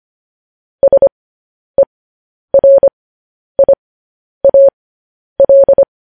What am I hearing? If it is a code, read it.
Morse code, heard as a keyed tone, about 25 wpm, spaced as for 10 wpm, 550 Hz. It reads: SERIAL
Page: news-headlines-in-morse-code-at-10-wpm